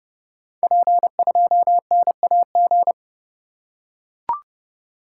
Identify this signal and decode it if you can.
Morse code, heard as a keyed tone, about 30 wpm, 700 Hz. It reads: P2NAG
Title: Code Groups - 5 Letter:Numbers - 30wpm